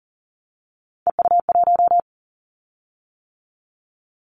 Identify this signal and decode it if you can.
Morse code, heard as a keyed tone, about 40 wpm, 700 Hz. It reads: EU1
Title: Callsign Prefixes for Top 50 DXCC - 40wpm